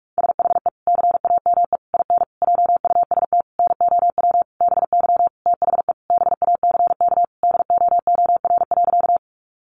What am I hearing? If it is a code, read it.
Morse code, heard as a keyed tone, about 44 wpm, 700 Hz. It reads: SHE CAME IN JUST NOW BY THE BACK DOOR.
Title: Sentences from Top 300 Words 45wpm